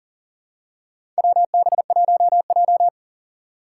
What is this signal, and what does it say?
Morse code, heard as a keyed tone, about 37 wpm, 700 Hz. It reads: WB1J
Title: USA Call Signs - 4 Characters - 3x Character Spacing - 40wpm